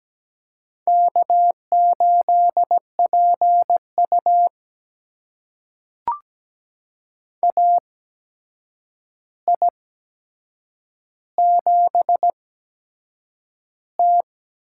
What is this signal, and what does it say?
Morse code, heard as a keyed tone, about 17 wpm, spaced as for 7 wpm, 700 Hz. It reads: K8PU AI7T
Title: USA Call Signs - 4 Characters - 8x Character Spacing - 17wpm